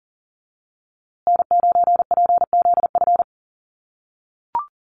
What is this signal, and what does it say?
Morse code, heard as a keyed tone, about 40 wpm, 700 Hz. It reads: N9PZF
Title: USA Call Signs Repeated 3x 40wpm